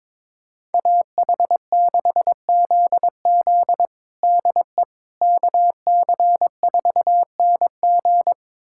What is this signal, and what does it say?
Morse code, heard as a keyed tone, about 22 wpm, 700 Hz. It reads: AH6ZZ DE KC4NG
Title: USA Callsign DE Callsign - 5 Characters - 22wpm